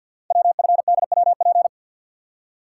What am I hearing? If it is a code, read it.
Morse code, heard as a keyed tone, about 48 wpm, 690 Hz. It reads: WFDWP
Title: Code Groups - 5 Letters - 50wpm